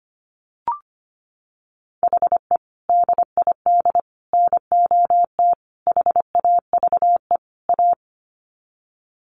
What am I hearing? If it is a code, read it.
Morse code, heard as a keyed tone, about 25 wpm, 700 Hz. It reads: HE DID NOT HAVE A
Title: Sets of 5 Words Encore 25wpm